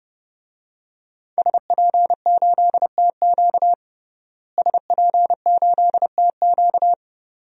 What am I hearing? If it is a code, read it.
Morse code, heard as a keyed tone, about 30 wpm, 700 Hz. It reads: SP8TQ SP8TQ
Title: DXCC-100 Call Signs - 5 Characters - Repeated 2x - 30wpm